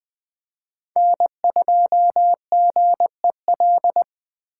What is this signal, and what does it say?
Morse code, heard as a keyed tone, about 20 wpm, 700 Hz. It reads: N2GEL
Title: Speed Racing - USA 1x3 Call Signs 17wpm